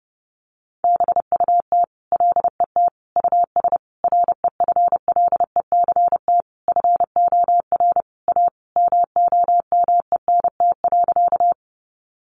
Morse code, heard as a keyed tone, about 30 wpm, 700 Hz. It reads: BUT LET US REFLECT FOR A MOMENT.
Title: Sentences from Top 1500 Words 30wpm